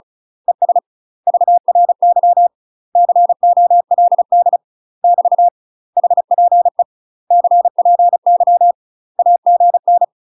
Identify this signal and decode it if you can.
Morse code, heard as a keyed tone, about 34 wpm, 700 Hz. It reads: ES VRY COLD <BT> HPE CPY AGN
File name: UK MASTER QSOs TRAINING / 35WPM / UK MASTER QSOs 0008